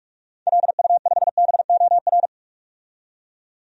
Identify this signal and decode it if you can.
Morse code, heard as a keyed tone, about 46 wpm, 700 Hz. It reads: LU5BOR
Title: DXCC-100 Call Signs - 6 Characters - 45wpm